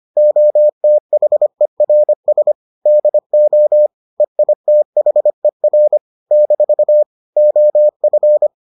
Morse code, heard as a keyed tone, about 25 wpm, 600 Hz. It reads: OTHERS DO EITHER - OF